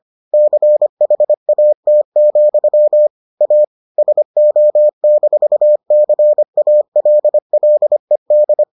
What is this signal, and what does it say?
Morse code, heard as a keyed tone, about 25 wpm, 600 Hz. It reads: CHAT, A SO-CALLED